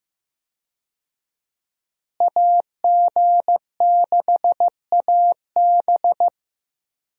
Morse code, heard as a keyed tone, about 15 wpm, 700 Hz. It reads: AG6AB
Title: USA Call Signs - 5 Characters - Repeated 3x - 15wpm